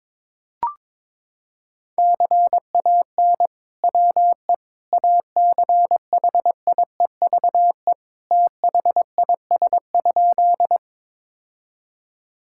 Morse code, heard as a keyed tone, about 22 wpm, 700 Hz. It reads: CAN WE ACHIEVE THIS?
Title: Sentences from Top 1000 Words - Review Part 6 - 22wpm